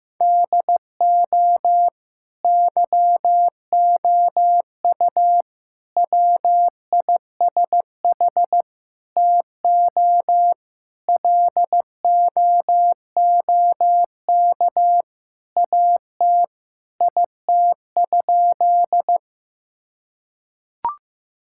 Morse code, heard as a keyed tone, about 15 wpm, 700 Hz. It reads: DO YOU WISH TO LOOK AT IT?
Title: Sentences from Top 1600 Words 15wpm